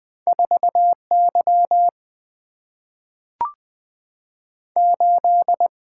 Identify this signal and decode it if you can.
Morse code, heard as a keyed tone, about 20 wpm, 700 Hz. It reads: 4 Y 8
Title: Single-Letter Single-Number 20wpm